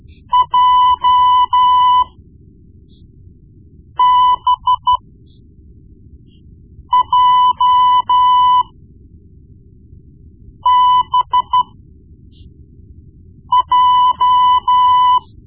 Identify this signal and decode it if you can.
Morse code, heard as a keyed tone, about 12 wpm, spaced as for 4 wpm, 1000 Hz. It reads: JBJBJ
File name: Morse Lessons / Morse Lessons / Tape-1B